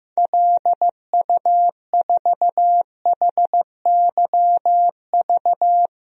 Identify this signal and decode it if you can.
Morse code, heard as a keyed tone, about 15 wpm, 700 Hz. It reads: LU4HYV
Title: DXCC-100 Call Signs - 6 Characters - 15wpm